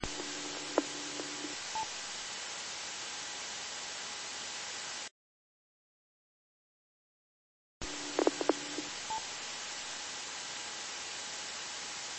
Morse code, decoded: none

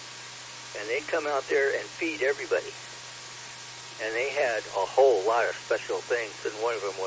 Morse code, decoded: none

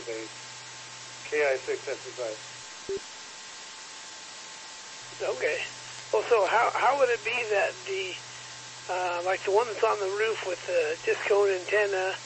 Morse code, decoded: none

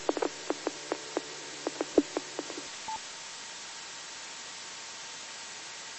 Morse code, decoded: none

